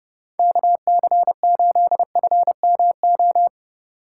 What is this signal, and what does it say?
Morse code, heard as a keyed tone, about 38 wpm, 700 Hz. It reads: KC8FMO